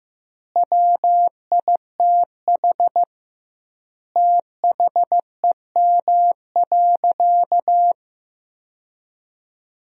Morse code, heard as a keyed tone, about 15 wpm, 700 Hz. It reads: WITH THEM.
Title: 2x Word-Spacing - Sentences from Top 200 Words - 15wpm